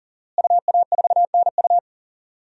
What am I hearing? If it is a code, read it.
Morse code, heard as a keyed tone, about 40 wpm, 700 Hz. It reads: UA4NU
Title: DXCC-100 Call Signs - 5 Characters - Repeated 3x - 40wpm